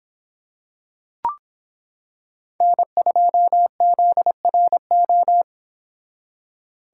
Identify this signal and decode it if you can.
Morse code, heard as a keyed tone, about 25 wpm, 700 Hz. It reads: N2ZRO